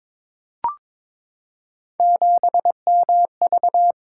Morse code, heard as a keyed tone, about 22 wpm, 700 Hz. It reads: 7MV
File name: practice / 22wpm / Three Letter-Number Combinations - 22wpm